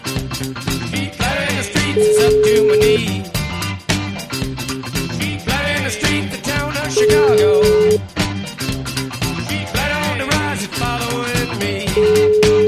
Ringback tone (Europe, 425 Hz), with noise.